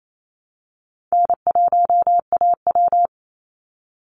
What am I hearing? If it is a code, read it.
Morse code, heard as a keyed tone, about 28 wpm, 700 Hz. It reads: N1AW
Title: NA SOTA Chaser Callsigns 28wpm